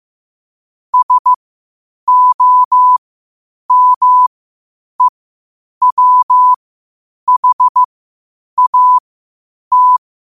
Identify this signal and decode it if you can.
Morse code, heard as a keyed tone, about 15 wpm, spaced as for 8 wpm, 1000 Hz. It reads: SOMEWHAT
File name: qotdfiles / 8wpm